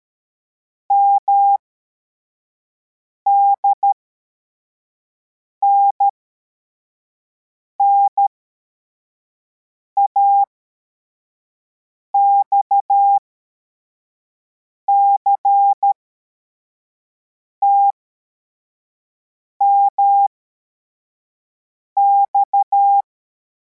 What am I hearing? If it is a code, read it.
Morse code, heard as a keyed tone, about 13 wpm, spaced as for 4 wpm, 800 Hz. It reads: MDNNA XCTMX
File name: formation / cw / Text34